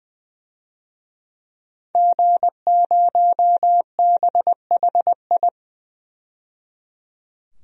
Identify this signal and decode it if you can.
Morse code, heard as a keyed tone, about 21 wpm, 700 Hz. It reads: G0BHI
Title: DXCC-100 Call Signs - 5 Characters - 20wpm